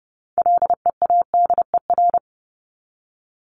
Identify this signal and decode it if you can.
Morse code, heard as a keyed tone, about 30 wpm, 700 Hz. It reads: LEADER
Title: Top 1501-1750 Words 30wpm